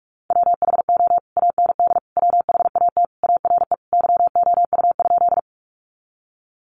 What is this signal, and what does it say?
Morse code, heard as a keyed tone, about 44 wpm, 700 Hz. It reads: WHO AND WHAT ARE YOU?
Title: Sentences from Top 100 Words 45wpm